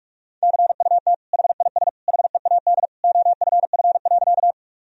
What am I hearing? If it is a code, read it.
Morse code, heard as a keyed tone, about 43 wpm, 700 Hz. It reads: CUT HIS HEAD OFF.
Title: Sentences from Top 1000 Words - Review Part 1 - 45wpm